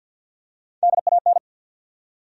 Morse code, heard as a keyed tone, about 48 wpm, 700 Hz. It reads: DAN
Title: USA Common Names - Max 3 Letters - 50wpm